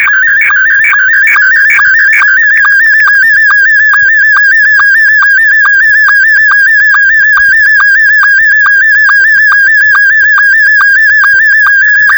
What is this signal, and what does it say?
SSTV (slow-scan television), Scottie 1.
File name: ba4hl / crsa / sstvCQ